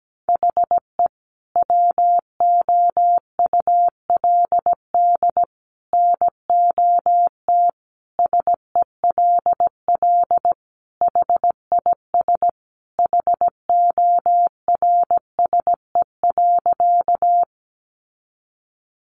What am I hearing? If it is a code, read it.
Morse code, heard as a keyed tone, about 17 wpm, 700 Hz. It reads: HE WOULD NOT SELL HIS HORSE.
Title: Sentences from Top 1100 Words 17wpm